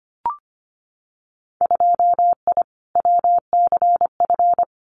Morse code, heard as a keyed tone, about 25 wpm, 700 Hz. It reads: 2I WCF
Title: Winter Field Day Exchanges 25wpm